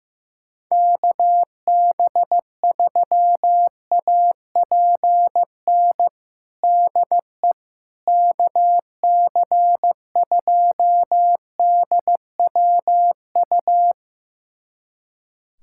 Morse code, heard as a keyed tone, about 15 wpm, 700 Hz. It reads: KB3APN DE KC2DWU
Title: USA Callsign DE Callsign - 6 Characters - 15wpm